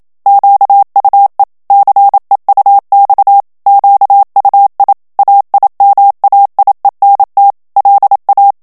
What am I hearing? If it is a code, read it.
Morse code, heard as a keyed tone, about 27 wpm, 800 Hz. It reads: QUE CEUX QUI AIMAIENT LA
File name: FRANCAIS TEXTE CLAIR 2 VITESSE 26 MOTS MINUTES